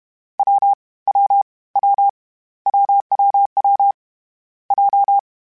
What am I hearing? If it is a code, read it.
Morse code, heard as a keyed tone, about 31 wpm, 800 Hz. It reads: W W W WWW J
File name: Short Runs / Short Runs-mp3s / AWJ @30